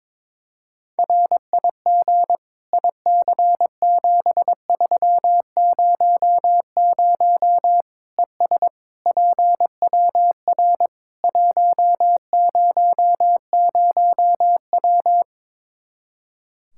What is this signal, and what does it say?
Morse code, heard as a keyed tone, about 22 wpm, 700 Hz. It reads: RIG IC7300 ES PWR 100W
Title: Practice QSO - Exchange 2 - Info - 22wpm